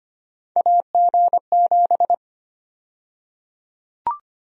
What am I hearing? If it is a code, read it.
Morse code, heard as a keyed tone, about 25 wpm, 700 Hz. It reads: AG7
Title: USA Call Sign Prefixes - Repeated 2x - 25wpm